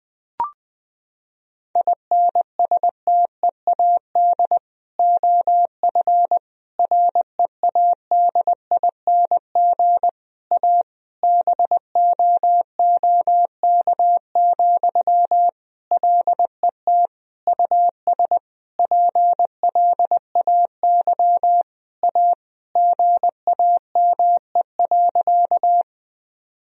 Morse code, heard as a keyed tone, about 20 wpm, 700 Hz. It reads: INSTEAD OF READING A BOOK, LET US PLAY A GAME.
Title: Sentences from Top 1000 Words - Review Part 4 - 20wpm